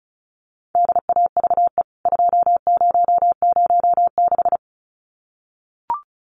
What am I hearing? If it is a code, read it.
Morse code, heard as a keyed tone, about 34 wpm, 700 Hz. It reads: DAVE 2006
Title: CWT Exchange 35wpm